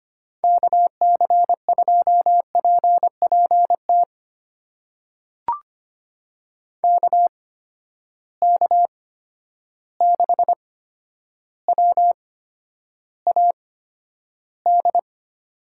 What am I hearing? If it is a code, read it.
Morse code, heard as a keyed tone, about 25 wpm, spaced as for 11 wpm, 700 Hz. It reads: KC2PPT KK6WAD
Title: USA Call Signs - 6 Characters - 8x Character Spacing - 25wpm